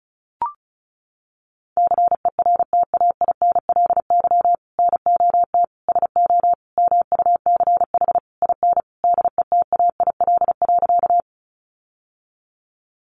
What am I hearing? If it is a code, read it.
Morse code, heard as a keyed tone, about 34 wpm, 700 Hz. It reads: CERTAINLY NOT SO MUCH IN DETAIL.
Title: Sentences from Top 1500 Words 35wpm